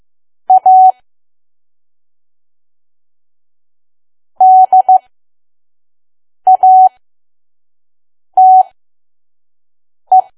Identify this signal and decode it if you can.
Morse code, heard as a keyed tone, about 15 wpm, spaced as for 5 wpm, 750 Hz. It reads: A DATE
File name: Archive / 5 WPM files / PM